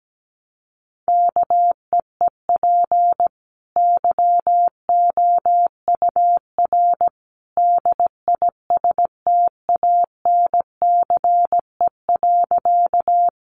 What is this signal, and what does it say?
Morse code, heard as a keyed tone, about 17 wpm, 700 Hz. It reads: KEEP YOUR DISTANCE.
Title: Sentences from Top 1200 Words 17wpm